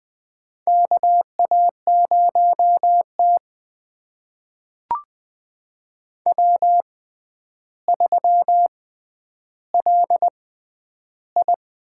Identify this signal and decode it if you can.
Morse code, heard as a keyed tone, about 20 wpm, spaced as for 10 wpm, 700 Hz. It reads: KA0T W3LI
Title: USA Call Signs - 4 Characters - 6x Character Spacing - 20wpm